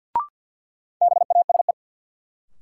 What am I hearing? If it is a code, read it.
Morse code, heard as a keyed tone, about 52 wpm, 700 Hz. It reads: BASE
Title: Sentences from Top 1000 Words - Review Part 1 - New Word Drill - 50wpm